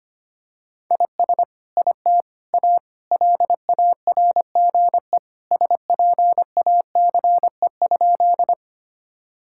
Morse code, heard as a keyed tone, about 25 wpm, 700 Hz. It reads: IS IT A LARGE SPACE?
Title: Sentences from Top 1000 Words - Review Part 5 - 25wpm